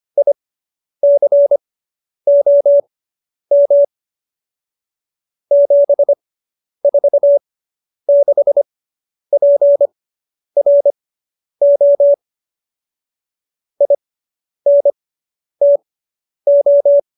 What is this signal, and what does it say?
Morse code, heard as a keyed tone, about 25 wpm, spaced as for 10 wpm, 570 Hz. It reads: ICOM 746PRO INTO